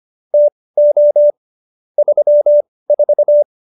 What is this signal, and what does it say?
Morse code, heard as a keyed tone, about 25 wpm, spaced as for 18 wpm, 590 Hz. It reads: TO 34